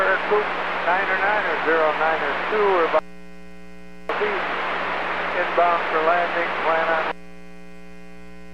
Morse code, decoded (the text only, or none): none